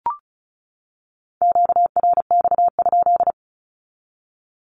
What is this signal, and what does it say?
Morse code, heard as a keyed tone, about 34 wpm, 700 Hz. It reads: QRX?
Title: Q-Codes 35wpm